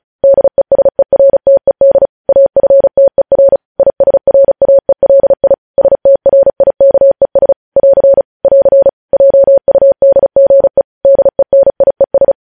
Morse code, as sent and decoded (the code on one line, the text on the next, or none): -.. ... .-. - . -.. / .- ..-. - . .-. / .. ... .-. .- . .-.. .. / ... - .-. .. -.- . ... / .-.-. / .-.-. / .--- ..- -.. --. / -.. -. .. ...
DESERTED AFTER ISRAELI STRIKES <AR> <AR> JUDGE DENIES